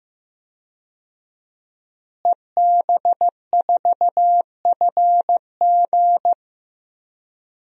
Morse code, heard as a keyed tone, about 15 wpm, 700 Hz. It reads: EB4FG
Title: DXCC-100 Call Signs - 5 Characters - Repeated 3x - 15wpm